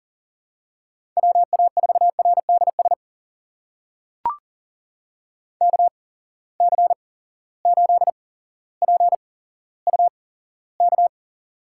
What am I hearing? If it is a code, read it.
Morse code, heard as a keyed tone, about 40 wpm, spaced as for 17 wpm, 700 Hz. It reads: WA4RDS KC8PUK
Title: USA Call Signs - 6 Characters - 8x Character Spacing - 40wpm